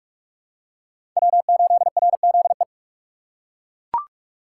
Morse code, heard as a keyed tone, about 41 wpm, 700 Hz. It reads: W8RZE
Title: USA Call Signs - 5 Characters - 2x Character Spacing - 45wpm